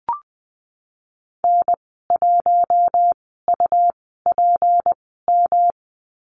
Morse code, heard as a keyed tone, about 20 wpm, spaced as for 15 wpm, 700 Hz. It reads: N1UPM
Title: 154 Lesson 25g - Letter p - Call Signs 20wpm